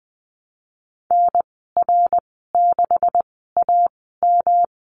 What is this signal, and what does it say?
Morse code, heard as a keyed tone, about 20 wpm, spaced as for 15 wpm, 700 Hz. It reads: NR6AM
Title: 161 Lesson 26g - Letter g - Call Signs 20wpm